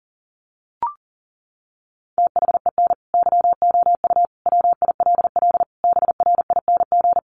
Morse code, heard as a keyed tone, about 40 wpm, 700 Hz. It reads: THEN YOU WILL BRING
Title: Sentences from Top 1500 Words 40wpm